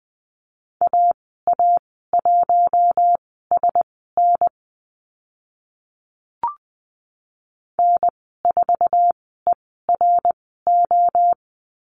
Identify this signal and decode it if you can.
Morse code, heard as a keyed tone, about 20 wpm, spaced as for 14 wpm, 700 Hz. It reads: AA1SN N4ERO
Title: 076 Lesson 13g - Letter L - Call Signs 20wpm